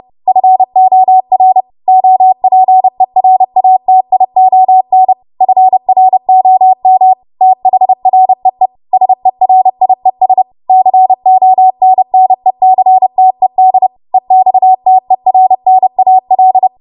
Morse code, heard as a keyed tone, about 30 wpm, 750 Hz. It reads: FOR OPERATION FROM THREE SERIES CONNECTED EXTERNAL